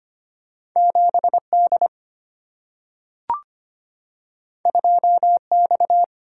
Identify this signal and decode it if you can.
Morse code, heard as a keyed tone, about 25 wpm, 700 Hz. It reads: 7D 2X
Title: Single-Letter Single-Number 25wpm